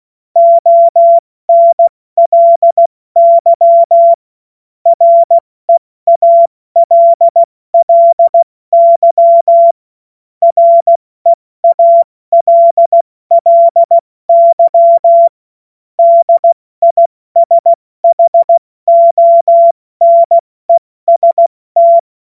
Morse code, heard as a keyed tone, about 16 wpm, 680 Hz. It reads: ONLY REALLY REALLY DISHONEST
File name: Morsekurs / Witze / Witz 27. September 2022 70 F80